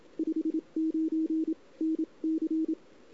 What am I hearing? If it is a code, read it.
Morse code, heard as a keyed tone, about 27 wpm, spaced as for 23 wpm, 320 Hz. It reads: 59 NC
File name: CW / 2008-ss-cw-k5tr-op-n5rz -03:00